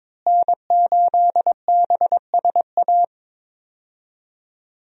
Morse code, heard as a keyed tone, about 22 wpm, 700 Hz. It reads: N8BSA
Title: USA Call Signs - 5 Characters - Repeated 3x - 22wpm